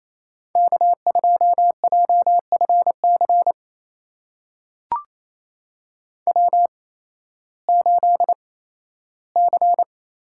Morse code, heard as a keyed tone, about 29 wpm, spaced as for 15 wpm, 700 Hz. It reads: K2JFC W8C
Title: USA Call Signs - 5 Characters - 8x Character Spacing - 28wpm